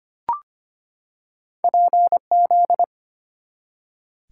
Morse code, heard as a keyed tone, about 25 wpm, 690 Hz. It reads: PZ
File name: Two Letter-Number Combinations - 25wpm